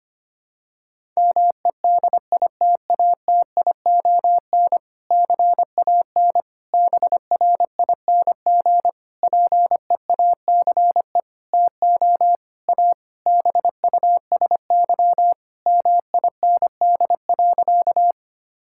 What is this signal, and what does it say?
Morse code, heard as a keyed tone, about 25 wpm, 700 Hz. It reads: MEDITATION CAN BRING PEACE TO A BUSY MIND.
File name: practice / 25wpm / Difficult Sentences with Hints - 25wpm